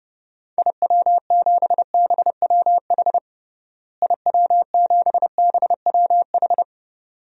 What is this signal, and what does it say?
Morse code, heard as a keyed tone, about 32 wpm, 700 Hz. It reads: IW7BWH IW7BWH
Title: DXCC-100 Call Signs - 6 Characters - Repeated 2x - 30wpm